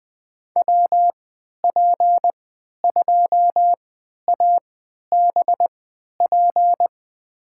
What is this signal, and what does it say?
Morse code, heard as a keyed tone, about 20 wpm, spaced as for 11 wpm, 700 Hz. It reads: WP2ABP
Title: USA Call Signs - 3x Character Spacing - 20wpm